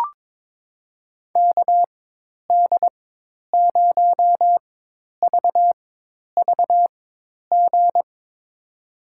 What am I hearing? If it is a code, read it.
Morse code, heard as a keyed tone, about 22 wpm, spaced as for 10 wpm, 700 Hz. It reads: KD0VVG
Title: USA Call Signs - 4x Character Spacing - 22wpm